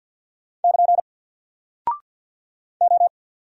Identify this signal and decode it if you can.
Morse code, heard as a keyed tone, about 50 wpm, 700 Hz. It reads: <KN> O
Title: Single Character (Letter:Number:Punctuation:Prosign) 50wpm